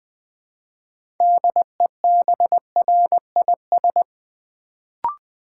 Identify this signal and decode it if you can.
Morse code, heard as a keyed tone, about 20 wpm, 700 Hz. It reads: DEBRIS
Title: Build A Word - 2 Letters At A Time - 6-Letter Words - 20wpm